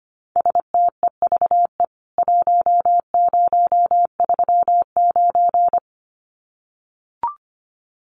Morse code, heard as a keyed tone, about 25 wpm, 700 Hz. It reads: STEVE 1039
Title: CWT Exchange 25wpm